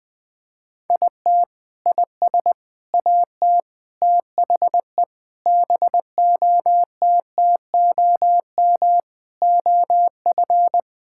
Morse code, heard as a keyed tone, about 20 wpm, 700 Hz. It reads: IT IS AT THE BOTTOM OF
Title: Sentences with 4-Word Phrases - Spoken Beginning - 20wpm